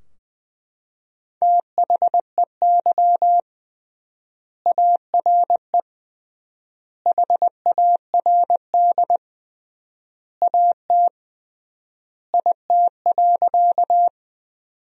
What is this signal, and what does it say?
Morse code, heard as a keyed tone, about 20 wpm, 700 Hz. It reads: THEY ARE HARD AT IT.
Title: 3x Word-Spacing - Sentences from Top 300 Words - 20wpm